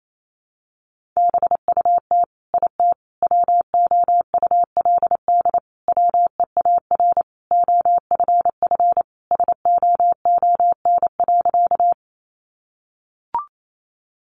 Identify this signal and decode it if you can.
Morse code, heard as a keyed tone, about 28 wpm, 700 Hz. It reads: BUT IT WOULD WEAR OFF SOON.
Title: Sentences from Top 1500 Words 28wpm